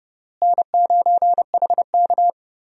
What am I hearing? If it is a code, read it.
Morse code, heard as a keyed tone, about 32 wpm, 700 Hz. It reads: N9HK
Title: USA Call Signs - 4 Characters - Repeated 3x - 30wpm